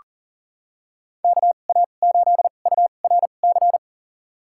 Morse code, heard as a keyed tone, about 40 wpm, spaced as for 29 wpm, 700 Hz. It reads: KA8URC
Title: USA Call Signs - 6 Characters - 2x Character Spacing - 40wpm